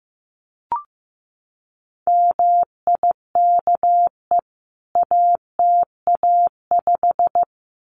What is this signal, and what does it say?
Morse code, heard as a keyed tone, about 15 wpm, 700 Hz. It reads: MIKE ATA5
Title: CWT Exchange 15wpm